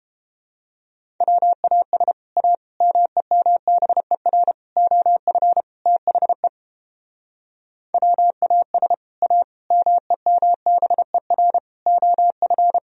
Morse code, heard as a keyed tone, about 31 wpm, 700 Hz. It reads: WAS A MEMBER OF THE WAS A MEMBER OF